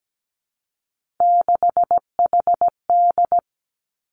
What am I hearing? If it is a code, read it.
Morse code, heard as a keyed tone, about 17 wpm, 700 Hz. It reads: 6HD
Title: Three Challenging Letter-Number Combinations 17wpm